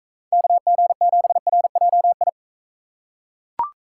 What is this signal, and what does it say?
Morse code, heard as a keyed tone, about 41 wpm, 700 Hz. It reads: KG7RJI